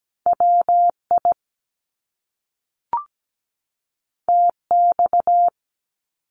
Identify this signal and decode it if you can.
Morse code, heard as a keyed tone, about 17 wpm, 700 Hz. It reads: WI TX
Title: US States 17wpm